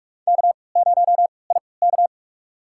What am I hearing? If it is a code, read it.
Morse code, heard as a keyed tone, about 46 wpm, spaced as for 25 wpm, 700 Hz. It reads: K0IK
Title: USA Call Signs - 4 Characters - 3x Character Spacing - 45wpm